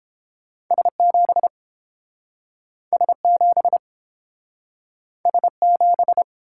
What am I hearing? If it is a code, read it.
Morse code, heard as a keyed tone, about 30 wpm, 700 Hz. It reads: S7 S7 S7